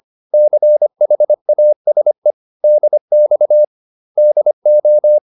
Morse code, heard as a keyed tone, about 25 wpm, 600 Hz. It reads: CHASE DX DO